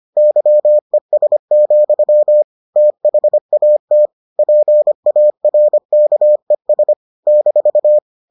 Morse code, heard as a keyed tone, about 25 wpm, 600 Hz. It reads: YES, THAT PARKES -